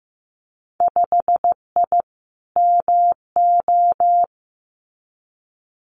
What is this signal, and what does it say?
Morse code, heard as a keyed tone, about 15 wpm, 700 Hz. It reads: 5I MO
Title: Winter Field Day Exchanges Encore 15wpm